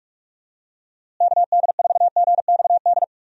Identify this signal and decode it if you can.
Morse code, heard as a keyed tone, about 46 wpm, 700 Hz. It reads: KD4GXD
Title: USA Call Signs - 6 Characters - Repeated 2x - 45wpm